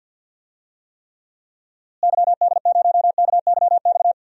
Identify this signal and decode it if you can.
Morse code, heard as a keyed tone, about 52 wpm, 700 Hz. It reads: YD0KYX
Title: DXCC-100 Call Signs - 6 Characters - Repeated 2x - 50wpm